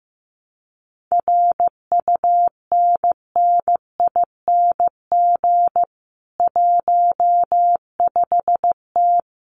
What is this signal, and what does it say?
Morse code, heard as a keyed tone, about 15 wpm, 700 Hz. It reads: RUNNING 15T
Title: Common QSO Phrases 15wpm